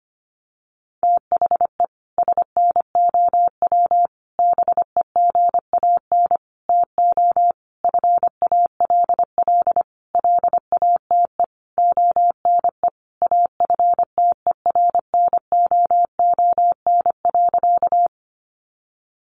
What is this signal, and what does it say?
Morse code, heard as a keyed tone, about 25 wpm, 700 Hz. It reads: THE SNOW BEGAN TO FALL LATE ONE AFTERNOON.
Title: Sentences from Top 1500 Words 25wpm